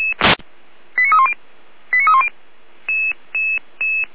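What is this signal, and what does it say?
Five-tone selective call (ZVEI), heard as quick runs of five tones, tones of about 70 ms. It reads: 89310 89310
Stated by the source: Five-tone ZVEI selective call received by SDR.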